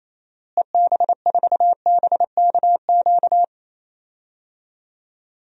Morse code, heard as a keyed tone, about 28 wpm, 700 Hz. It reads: EB4BKQ
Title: DXCC-100 Call Signs - 6 Characters - Repeated 2x - 28wpm